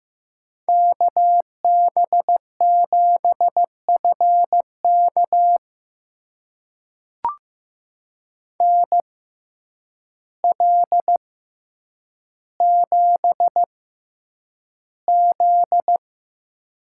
Morse code, heard as a keyed tone, about 15 wpm, spaced as for 8 wpm, 700 Hz. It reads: KB7FK NL7Z
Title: USA Call Signs - 6x Character Spacing - 15wpm